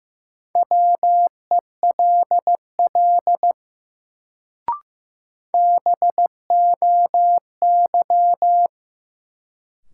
Morse code, heard as a keyed tone, about 15 wpm, 700 Hz. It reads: WELL BOY
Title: Sentences from Top 200 Words - New Word Drill - 15wpm